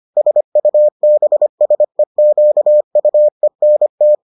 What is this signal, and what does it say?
Morse code, heard as a keyed tone, about 25 wpm, 600 Hz. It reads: SUBSEQUENT